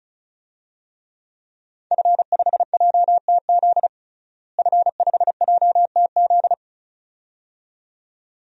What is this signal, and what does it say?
Morse code, heard as a keyed tone, about 34 wpm, 700 Hz. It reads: F5JTZ F5JTZ
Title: DXCC-100 Call Signs - 5 Characters - Repeated 2x - 35wpm